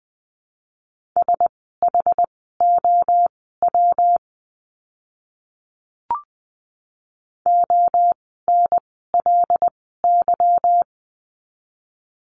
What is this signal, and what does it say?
Morse code, heard as a keyed tone, about 19 wpm, spaced as for 14 wpm, 700 Hz. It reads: SHOW ONLY
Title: 192 Lesson 31f - Letter v - Words 20wpm